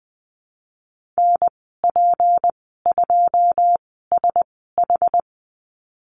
Morse code, heard as a keyed tone, about 20 wpm, spaced as for 15 wpm, 700 Hz. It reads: NP2SH
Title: 193 Lesson 31g - Letter v - Call Signs 20wpm